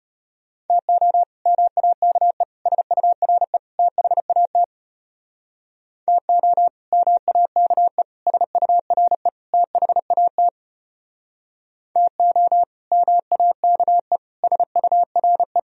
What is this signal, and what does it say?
Morse code, heard as a keyed tone, about 33 wpm, 700 Hz. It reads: TO MAKE SURE THAT TO MAKE SURE THAT TO MAKE SURE